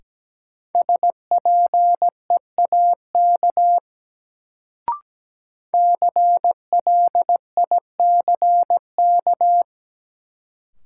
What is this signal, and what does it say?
Morse code, heard as a keyed tone, about 17 wpm, 700 Hz. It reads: SPEAK CLICK
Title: Sentences from Top 1100 Words - New Word Drill Alt - 17wpm